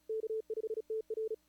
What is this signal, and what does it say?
Morse code, heard as a keyed tone, about 34 wpm, 440 Hz. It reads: K5TR
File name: CWSprint / Feb2022 / k5tr-feb-na-sprint-cw-k5ot -0000